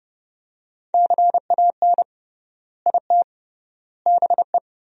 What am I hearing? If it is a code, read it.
Morse code, heard as a keyed tone, about 30 wpm, 700 Hz. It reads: CAN IT BE
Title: 3x Word-Spacing - Sentences from Top 500 Words - 30wpm